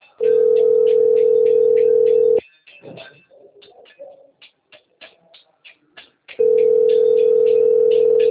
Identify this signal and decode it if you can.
Ringback tone (North America), 440 + 480 Hz.
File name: rg-1600-unknown- - - .18548